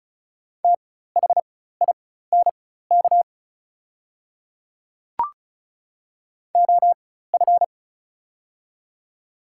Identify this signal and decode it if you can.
Morse code, heard as a keyed tone, about 36 wpm, spaced as for 16 wpm, 700 Hz. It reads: THINK OF
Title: 4x Character Spacing - Top 200 Words - 35wpm